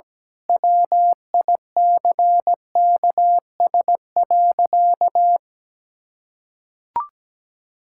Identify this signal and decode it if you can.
Morse code, heard as a keyed tone, about 17 wpm, 700 Hz. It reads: WICKS.
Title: Pangram Challenge v2 - 17wpm